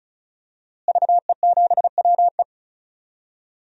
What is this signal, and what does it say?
Morse code, heard as a keyed tone, about 36 wpm, 700 Hz. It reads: VE7WE
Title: DXCC-100 Call Signs - 5 Characters - Repeated 3x - 35wpm